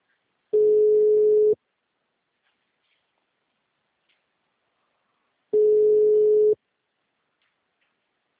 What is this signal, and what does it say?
Ringback tone (Europe, 425 Hz).